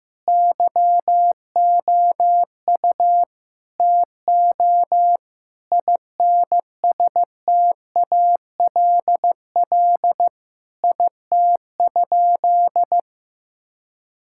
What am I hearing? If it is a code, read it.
Morse code, heard as a keyed tone, about 16 wpm, 700 Hz. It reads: YOU TO INSTALL IT?
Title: Sentences from Top 1600 Words 15wpm